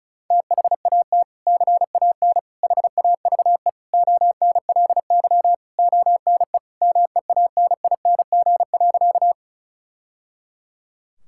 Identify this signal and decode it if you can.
Morse code, heard as a keyed tone, about 36 wpm, 700 Hz. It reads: THAT CAN HAVE ONLY ONE MEANING.
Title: Sentences from Top 1000 Words - Review Part 6 - 35wpm